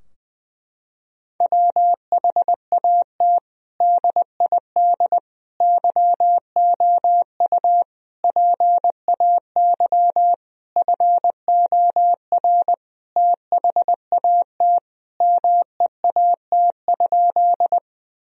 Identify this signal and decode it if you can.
Morse code, heard as a keyed tone, about 20 wpm, 700 Hz. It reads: WHAT DID YOU PAY FOR THAT MEAT?
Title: Sentences from Top 1000 Words - Review Part 3 - 20wpm